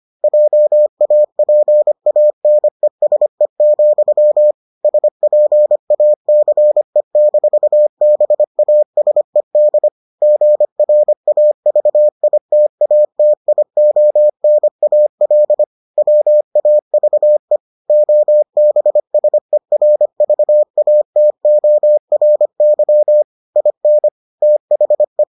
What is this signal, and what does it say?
Morse code, heard as a keyed tone, about 25 wpm, 600 Hz. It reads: JAPANESE, SPACE-BASED GRAVITATIONAL WAVE OBSERVATORY IN THE